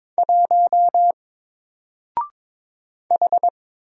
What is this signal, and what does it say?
Morse code, heard as a keyed tone, about 22 wpm, 700 Hz. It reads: 1 H
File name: Single-Letter-Number-22wpm